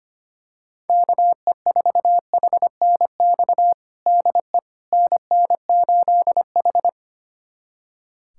Morse code, heard as a keyed tone, about 25 wpm, 700 Hz. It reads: KE4HNX DE NN8H
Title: USA Callsign DE Callsign 25wpm